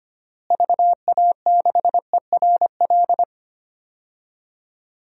Morse code, heard as a keyed tone, about 25 wpm, 700 Hz. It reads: VA6ERL
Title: DXCC-100 Call Signs - 6 Characters - Repeated 2x - 25wpm